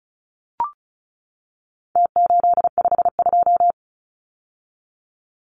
Morse code, heard as a keyed tone, about 34 wpm, 700 Hz. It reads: T852
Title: Numbers - Contest Style - 4-Digit 35wpm